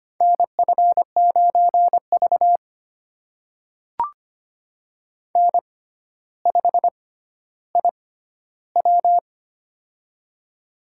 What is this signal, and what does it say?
Morse code, heard as a keyed tone, about 25 wpm, spaced as for 13 wpm, 700 Hz. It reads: NF9V N5IW
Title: USA Call Signs - 4 Characters - 6x Character Spacing - 25wpm